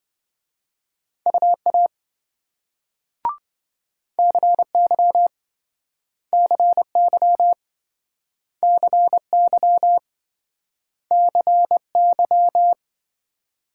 Morse code, heard as a keyed tone, about 28 wpm, 700 Hz. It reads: UA CY CY CY CY